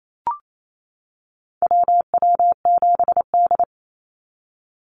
Morse code, heard as a keyed tone, about 28 wpm, 700 Hz. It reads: WW7D
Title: NA SOTA Activator Callsigns 28wpm